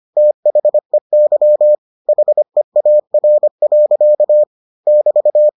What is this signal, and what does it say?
Morse code, heard as a keyed tone, about 25 wpm, 600 Hz. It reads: THEY HEAR. <BT>